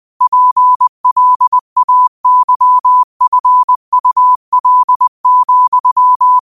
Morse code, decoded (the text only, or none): PLAYFUL,